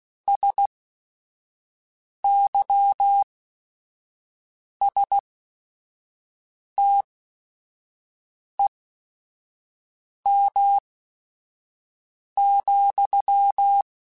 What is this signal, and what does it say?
Morse code, heard as a keyed tone, about 16 wpm, spaced as for 5 wpm, 800 Hz. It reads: SYSTEM,